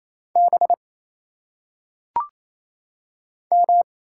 Morse code, heard as a keyed tone, about 27 wpm, 700 Hz. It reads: B M
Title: Single Letters - Phonetic Pronunciation 28wpm